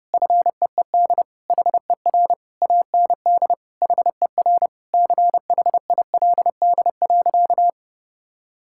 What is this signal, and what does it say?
Morse code, heard as a keyed tone, about 30 wpm, 700 Hz. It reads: FEED HER AND HER CHILD.
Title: Sentences from Top 1600 Words 30wpm